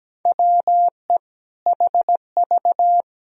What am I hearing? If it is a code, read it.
Morse code, heard as a keyed tone, about 17 wpm, 700 Hz. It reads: WE HV
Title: Sentences from Top 100 Words and Abbreviations 17wpm